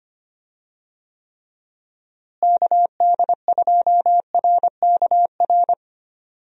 Morse code, heard as a keyed tone, about 25 wpm, 700 Hz. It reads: KD2RKR
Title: USA Call Signs - 6 Characters - Repeated 2x - 25wpm